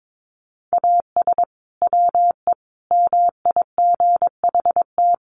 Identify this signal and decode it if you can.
Morse code, heard as a keyed tone, about 22 wpm, 700 Hz. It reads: AS WE MIGHT
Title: Sentences from Top 1100 Words 22wpm